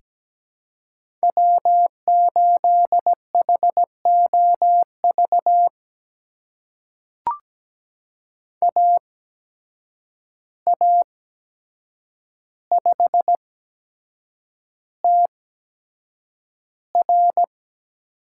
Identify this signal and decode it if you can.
Morse code, heard as a keyed tone, about 17 wpm, spaced as for 7 wpm, 700 Hz. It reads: W8HOV AA5TR
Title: USA Call Signs - 5 Characters - 8x Character Spacing - 17wpm